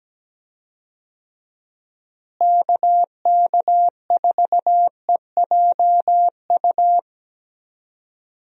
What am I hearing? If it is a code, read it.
Morse code, heard as a keyed tone, about 17 wpm, 700 Hz. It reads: KK4EJU
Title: USA Call Signs - 6 Characters - Repeated 3x - 17wpm